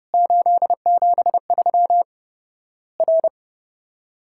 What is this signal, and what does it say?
Morse code, heard as a keyed tone, about 30 wpm, 700 Hz. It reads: 873 R
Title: Worked All Europe DX Contest - QTC Exchange - 30wpm